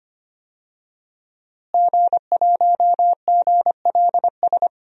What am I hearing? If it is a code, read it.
Morse code, heard as a keyed tone, about 25 wpm, 700 Hz. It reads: G1GLS
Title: DXCC-100 Call Signs - 5 Characters - Repeated 2x - 25wpm